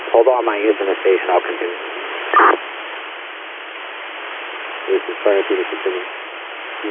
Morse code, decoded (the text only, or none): none